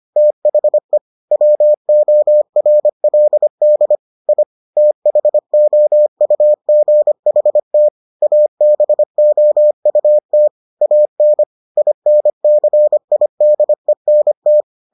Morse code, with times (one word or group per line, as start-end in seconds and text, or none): 0.2-1.0 THE
1.3-4.0 WORLD
4.3-4.4 I
4.8-7.9 THOUGHT
8.2-10.5 ABOUT
10.8-11.4 AN
11.8-14.6 INCIDENT